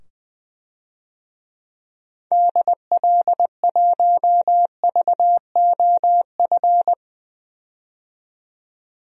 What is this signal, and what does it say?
Morse code, heard as a keyed tone, about 21 wpm, 700 Hz. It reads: DL1VOF
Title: DXCC-100 Call Signs - 6 Characters - 20wpm